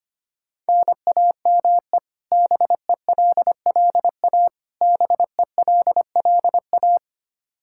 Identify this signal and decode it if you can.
Morse code, heard as a keyed tone, about 25 wpm, 700 Hz. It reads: NAME BELLA BELLA
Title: Practice QSO - Exchange 1 - Beginning - 25wpm